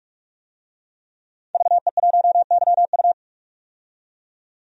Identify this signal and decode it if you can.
Morse code, heard as a keyed tone, about 42 wpm, 700 Hz. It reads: VE1YU
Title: DXCC-100 Call Signs - 5 Characters - 45wpm